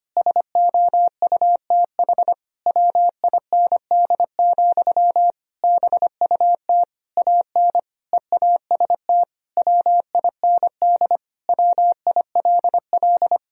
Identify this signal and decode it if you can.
Morse code, heard as a keyed tone, about 25 wpm, 700 Hz. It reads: SOUTH WIND, BUT AN EAST WIND WILL
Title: Sentences from Top 1000 Words - Review Part 3 - 25wpm